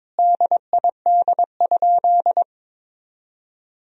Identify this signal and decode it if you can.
Morse code, heard as a keyed tone, about 22 wpm, 700 Hz. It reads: DID?
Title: 3x Word-Spacing - Sentences from Top 100 Words - 22wpm